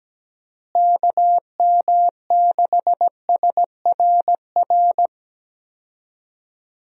Morse code, heard as a keyed tone, about 17 wpm, 700 Hz. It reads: KM6SRR
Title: USA Call Signs - 6 Characters - Repeated 3x - 17wpm